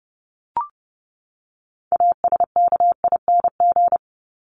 Morse code, heard as a keyed tone, about 30 wpm, 700 Hz. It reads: ASKING
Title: ING Suffix - Top 100 Words with Base Words 30wpm